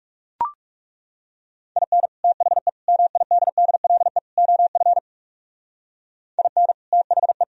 Morse code, heard as a keyed tone, about 38 wpm, 700 Hz. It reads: IN THE MIDDLE OF IN THE
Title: Speed-Racing - Sets of 4 Words 30wpm